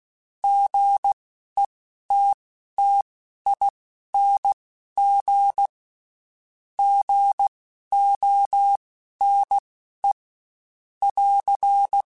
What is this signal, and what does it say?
Morse code, heard as a keyed tone, about 16 wpm, spaced as for 11 wpm, 800 Hz. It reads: GETTING GONE <AR>
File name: Short Runs / Short Runs-mp3s / Gotten @15